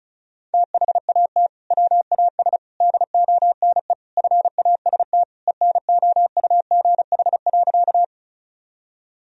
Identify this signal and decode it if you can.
Morse code, heard as a keyed tone, about 36 wpm, 700 Hz. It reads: THAT WAS DONE FAST ENOUGH.
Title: Sentences from Top 1000 Words - Review Part 2 - 35wpm